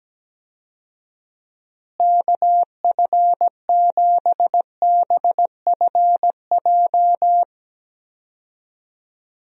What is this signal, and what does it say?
Morse code, heard as a keyed tone, about 17 wpm, 700 Hz. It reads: KF7BFJ
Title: USA Call Signs - 6 Characters - Repeated 2x - 17wpm